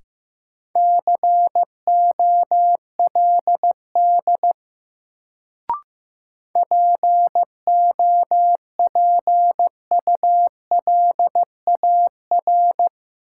Morse code, heard as a keyed tone, about 15 wpm, 700 Hz. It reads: COLD POPULAR
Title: Sets of 5 Words - New Word Drill - 15wpm